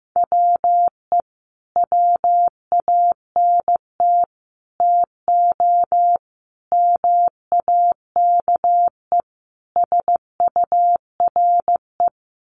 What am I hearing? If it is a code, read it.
Morse code, heard as a keyed tone, about 15 wpm, 700 Hz. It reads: WE WANT TO MAKE SURE
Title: Sets of 5 Words Encore2 15wpm